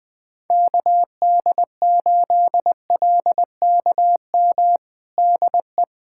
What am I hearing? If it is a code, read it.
Morse code, heard as a keyed tone, about 21 wpm, 700 Hz. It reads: KD8LKM DE
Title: USA Callsign DE Callsign - 6 Characters - 20wpm